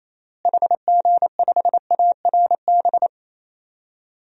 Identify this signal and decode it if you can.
Morse code, heard as a keyed tone, about 28 wpm, 700 Hz. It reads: HG5ARB
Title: DXCC-100 Call Signs - 6 Characters - Repeated 2x - 28wpm